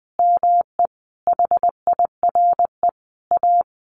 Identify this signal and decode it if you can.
Morse code, heard as a keyed tone, about 19 wpm, 700 Hz. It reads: ME HIRE A
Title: Sentences from Top 1400 Words 20wpm